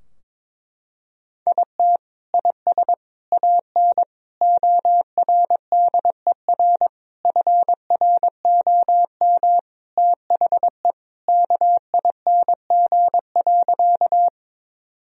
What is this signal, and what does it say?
Morse code, heard as a keyed tone, about 22 wpm, 700 Hz. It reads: IT IS AN ORDER FROM THE KING.
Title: Sentences from Top 1000 Words - Review Part 2 - 22wpm